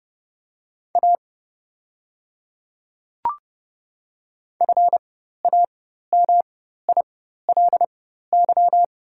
Morse code, heard as a keyed tone, about 30 wpm, spaced as for 14 wpm, 700 Hz. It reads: A FAMILY
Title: 4x Character Spacing - Top 500 Words - 30wpm